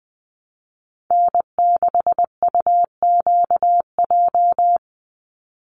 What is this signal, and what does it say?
Morse code, heard as a keyed tone, about 20 wpm, 700 Hz. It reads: N6UQJ
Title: USA Call Signs 20wpm